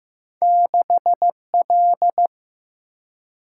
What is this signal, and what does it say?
Morse code, heard as a keyed tone, about 15 wpm, 700 Hz. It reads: 6 L
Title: Single-Letter Single-Number 15wpm